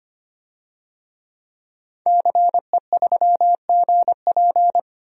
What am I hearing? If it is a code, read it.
Morse code, heard as a keyed tone, about 25 wpm, 700 Hz. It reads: CE3GP
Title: DXCC-100 Call Signs - 5 Characters - 25wpm